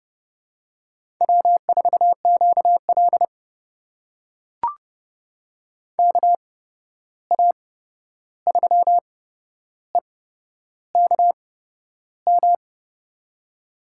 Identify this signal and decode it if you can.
Morse code, heard as a keyed tone, about 30 wpm, spaced as for 11 wpm, 700 Hz. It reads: W4QL KA3EKM
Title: USA Call Signs - 8x Character Spacing - 30wpm